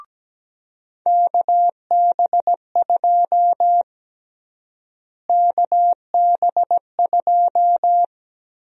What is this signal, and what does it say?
Morse code, heard as a keyed tone, about 17 wpm, 700 Hz. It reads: KB2 KB2
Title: USA Call Sign Prefixes - Repeated 2x - 17wpm